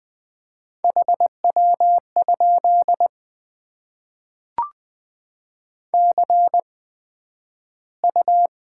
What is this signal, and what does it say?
Morse code, heard as a keyed tone, about 20 wpm, spaced as for 11 wpm, 700 Hz. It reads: HW? CU
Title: Top 20 QSO Elements - 8x Character Spacing - 20wpm